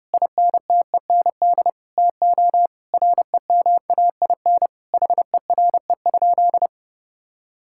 Morse code, heard as a keyed tone, about 30 wpm, 700 Hz. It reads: INTEND TO REMAIN HERE?
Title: Sentences from Top 1600 Words 30wpm